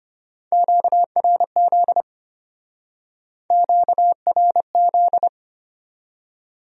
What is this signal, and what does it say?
Morse code, heard as a keyed tone, about 33 wpm, 700 Hz. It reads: QRZ QRZ